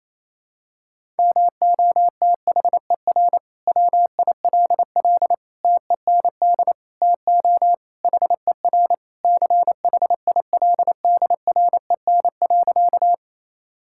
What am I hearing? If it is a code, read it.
Morse code, heard as a keyed tone, about 28 wpm, 700 Hz. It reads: MOTHER WILL TEND TO HER CHILDREN.
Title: Sentences from Top 1000 Words - Review Part 5 - 28wpm